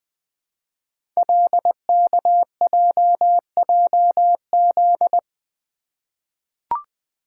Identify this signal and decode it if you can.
Morse code, heard as a keyed tone, about 20 wpm, 700 Hz. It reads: LKJJZ
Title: Code Groups - 5 Letters - 20wpm